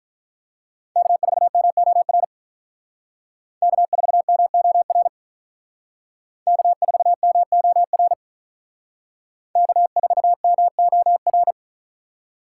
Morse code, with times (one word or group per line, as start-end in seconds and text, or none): 1.0-2.2 K4MOR
3.6-5.1 K4MOR
6.5-8.1 K4MOR
9.6-11.5 K4MOR